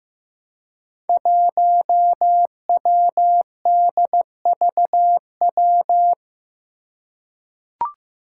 Morse code, heard as a keyed tone, about 15 wpm, 700 Hz. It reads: 1WDVW
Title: Five Challenging Letter-Number Combinations 15wpm